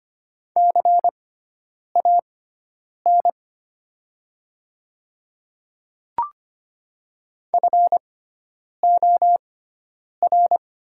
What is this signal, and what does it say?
Morse code, heard as a keyed tone, about 25 wpm, spaced as for 9 wpm, 700 Hz. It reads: CAN FOR
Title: 6x Character Spacing - Top 100 Words - 25wpm